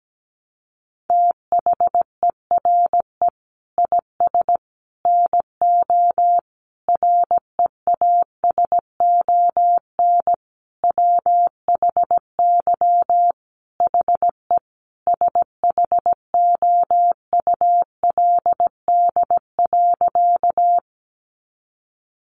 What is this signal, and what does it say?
Morse code, heard as a keyed tone, about 17 wpm, 700 Hz. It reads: THERE IS NO REASON WHY HE SHOULD.
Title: Sentences from Top 700 Words 17wpm